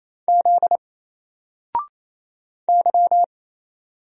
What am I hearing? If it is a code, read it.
Morse code, heard as a keyed tone, about 28 wpm, 700 Hz. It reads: Z Y
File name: Single-Letter-Number-28wpm